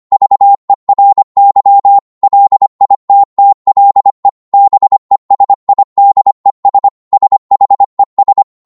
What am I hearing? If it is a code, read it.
Morse code, heard as a keyed tone, about 25 wpm, 800 Hz. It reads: VERY LITTLE BESIDES SHES